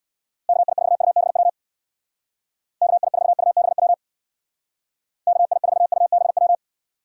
Morse code, heard as a keyed tone, about 66 wpm, 700 Hz. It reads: KI4UBW KI4UBW KI4UBW